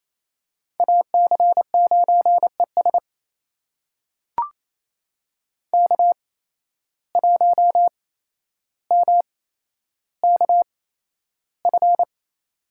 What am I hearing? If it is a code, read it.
Morse code, heard as a keyed tone, about 28 wpm, spaced as for 12 wpm, 700 Hz. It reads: AC9ES K1MKF
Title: USA Call Signs - 5 Characters - 8x Character Spacing - 28wpm